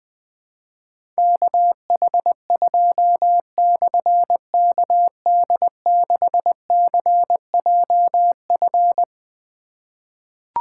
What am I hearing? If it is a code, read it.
Morse code, heard as a keyed tone, about 20 wpm, 700 Hz. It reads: KH2/KD6CJF
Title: Difficult Call Signs 20wpm